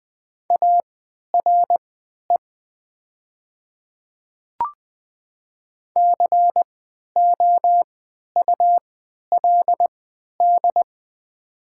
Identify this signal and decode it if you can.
Morse code, heard as a keyed tone, about 20 wpm, spaced as for 11 wpm, 700 Hz. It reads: ARE COULD
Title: 3x Character Spacing - Top 100 Words - 20wpm